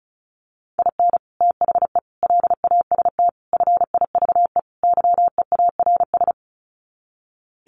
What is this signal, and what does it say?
Morse code, heard as a keyed tone, about 34 wpm, 700 Hz. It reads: IN THE LAST FIVE YEARS
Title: Sets of 5 Words Encore 35wpm